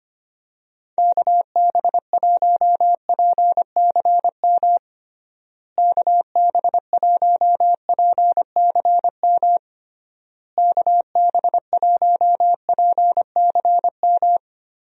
Morse code, heard as a keyed tone, about 25 wpm, 700 Hz. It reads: KB1PCM KB1PCM KB1PCM
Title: USA Call Signs - 6 Characters - Repeated 3x - 25wpm